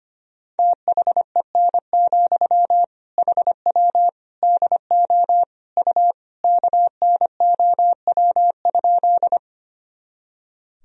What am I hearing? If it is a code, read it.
Morse code, heard as a keyed tone, about 25 wpm, 700 Hz. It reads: THEN, HW DO U KNOW?
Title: Sentences from Top 100 Words and Abbreviations 25wpm